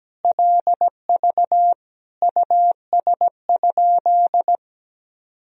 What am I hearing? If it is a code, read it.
Morse code, heard as a keyed tone, about 17 wpm, 700 Hz. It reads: LV US?
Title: Sentences from Top 400 Words and Abbreviations 17wpm